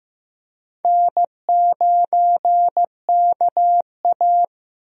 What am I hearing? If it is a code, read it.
Morse code, heard as a keyed tone, about 15 wpm, 700 Hz. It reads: N9KA
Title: USA Call Signs - 4 Characters - 15wpm